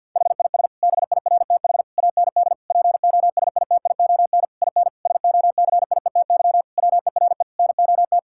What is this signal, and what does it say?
Morse code, heard as a keyed tone, about 47 wpm, 700 Hz. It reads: HIS BIRTH AND POSITION IN SOCIETY WERE NOT
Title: Sentences from Top 1000 Words - Review Part 5 - 50wpm